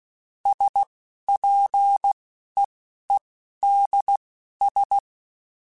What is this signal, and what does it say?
Morse code, heard as a keyed tone, about 16 wpm, spaced as for 11 wpm, 800 Hz. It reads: SPEEDS
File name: Spaced Characters / Spaced Characters-mp3s / Weather4 10 sp @15